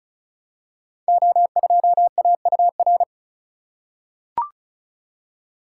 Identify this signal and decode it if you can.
Morse code, heard as a keyed tone, about 36 wpm, 700 Hz. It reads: O2AUR
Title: Code Groups - 5 Letter:Numbers - 35wpm